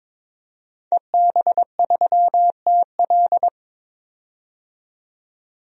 Morse code, heard as a keyed tone, about 22 wpm, 700 Hz. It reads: EB3TL
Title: DXCC-100 Call Signs - 5 Characters - 22wpm